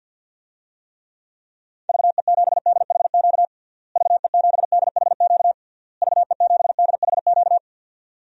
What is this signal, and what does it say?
Morse code, heard as a keyed tone, about 52 wpm, 700 Hz. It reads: VE7DHQ VE7DHQ VE7DHQ
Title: DXCC-100 Call Signs - 6 Characters - Repeated 3x - 50wpm